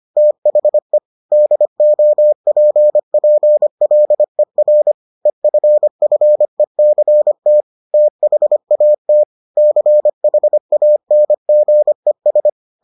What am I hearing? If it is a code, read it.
Morse code, heard as a keyed tone, about 25 wpm, 600 Hz. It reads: THE DOPPLER EFFECT THAT CHANGES